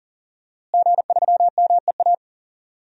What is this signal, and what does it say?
Morse code, heard as a keyed tone, about 40 wpm, 700 Hz. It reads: G3MEA
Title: DXCC-100 Call Signs - 5 Characters - Repeated 2x - 40wpm